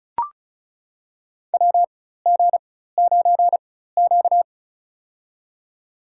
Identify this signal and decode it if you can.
Morse code, heard as a keyed tone, about 36 wpm, spaced as for 16 wpm, 700 Hz. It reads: WG9Q
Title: USA Call Signs - 4 Characters - 4x Character Spacing - 35wpm